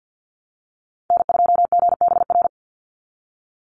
Morse code, heard as a keyed tone, about 50 wpm, 700 Hz. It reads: N2GBR
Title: NA SOTA Chaser Callsigns 50wpm